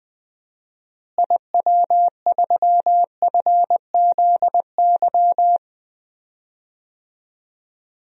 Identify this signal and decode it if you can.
Morse code, heard as a keyed tone, about 21 wpm, 700 Hz. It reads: IW3FZY
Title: DXCC-100 Call Signs - 6 Characters - Repeated 3x - 20wpm